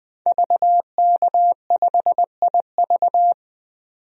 Morse code, heard as a keyed tone, about 20 wpm, 700 Hz. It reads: VK5IV
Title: DXCC-100 Call Signs - 5 Characters - 20wpm